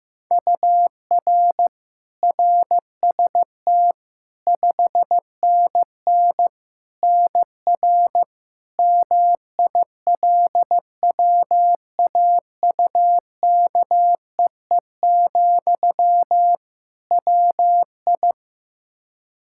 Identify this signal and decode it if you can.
Morse code, heard as a keyed tone, about 15 wpm, 700 Hz. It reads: UR RST 5NN NR MILWAUKEE, WI
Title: Practice QSO - Exchange 1 - Beginning - 15wpm